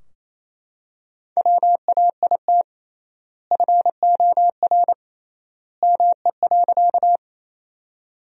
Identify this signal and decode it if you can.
Morse code, heard as a keyed tone, about 27 wpm, 700 Hz. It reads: WAIT FOR ME.
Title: 3x Word-Spacing - Sentences from Top 500 Words - 28wpm